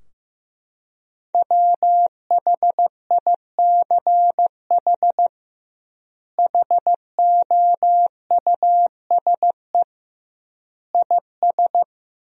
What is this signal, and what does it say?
Morse code, heard as a keyed tone, about 15 wpm, 700 Hz. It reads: WHICH HOUSE IS
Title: 2x Word-Spacing - Sentences from Top 200 Words - 15wpm